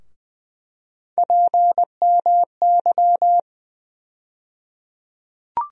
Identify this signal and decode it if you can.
Morse code, heard as a keyed tone, about 20 wpm, 700 Hz. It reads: PMY
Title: USA Call Sign Postfixes - 3 Letters 20wpm